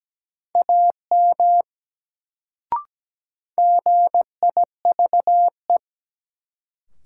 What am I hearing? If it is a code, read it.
Morse code, heard as a keyed tone, about 17 wpm, 700 Hz. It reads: AM GIVE
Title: Sets of 2 Words - New Word Drill - 17wpm